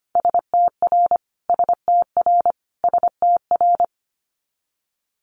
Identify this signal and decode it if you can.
Morse code, heard as a keyed tone, about 25 wpm, 700 Hz. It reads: STR STR STR
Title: Multi-letter Phonemes Repeated 3x 25wpm